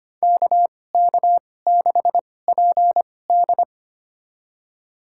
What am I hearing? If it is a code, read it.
Morse code, heard as a keyed tone, about 25 wpm, spaced as for 18 wpm, 700 Hz. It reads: KK6PD
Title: USA Call Signs - 5 Characters - 2x Character Spacing - 25wpm